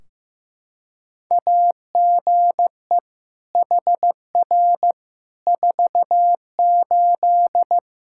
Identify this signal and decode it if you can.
Morse code, heard as a keyed tone, about 15 wpm, 700 Hz. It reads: AGE HR 48
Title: Practice QSO - Exchange 3 - More Info - 15wpm